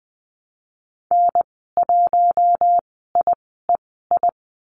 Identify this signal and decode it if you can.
Morse code, heard as a keyed tone, about 20 wpm, spaced as for 14 wpm, 700 Hz. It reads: N1IEI
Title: 042 Lesson 8g - Number 1 - Call Signs 20wpm